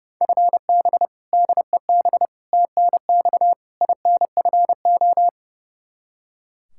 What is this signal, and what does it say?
Morse code, heard as a keyed tone, about 30 wpm, 700 Hz. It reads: FB DEB TNX INFO
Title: K1USN SST QSO Practice 30wpm